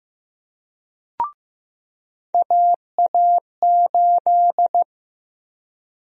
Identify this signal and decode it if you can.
Morse code, heard as a keyed tone, about 15 wpm, 700 Hz. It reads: AA8
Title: USA Call Sign Prefixes - 15wpm